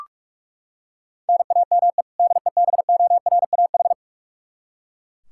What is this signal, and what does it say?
Morse code, heard as a keyed tone, about 41 wpm, 700 Hz. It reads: NAME DEBORAH
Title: Practice QSO - Exchange 1 - Beginning - 45wpm